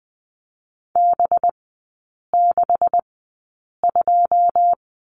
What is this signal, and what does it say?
Morse code, heard as a keyed tone, about 20 wpm, spaced as for 8 wpm, 700 Hz. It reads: B62
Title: 184 Lesson 30e - Letter b - Three Characters 20wpm